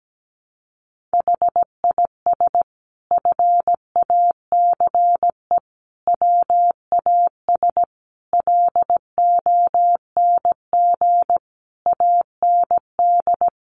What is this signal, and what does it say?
Morse code, heard as a keyed tone, about 17 wpm, 700 Hz. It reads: HIS FACE WAS LONG AND
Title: Sentences from Top 1200 Words 17wpm